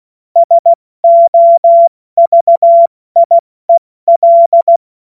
Morse code, heard as a keyed tone, about 16 wpm, 680 Hz. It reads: SOVIEL
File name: Morsekurs / Witze / Witz 25. Oktober 2022 70 F80